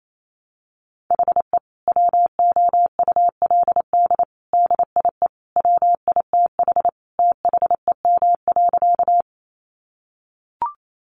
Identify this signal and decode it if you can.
Morse code, heard as a keyed tone, about 28 wpm, 700 Hz. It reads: HE WOULD DIE WITH THEM.
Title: Sentences from Top 1300 Words 28wpm